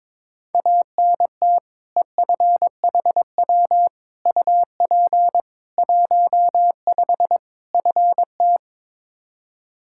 Morse code, heard as a keyed tone, about 23 wpm, 700 Hz. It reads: ANT EFHW UP 15 FT
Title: Practice QSO - Exchange 2 - Info - 22wpm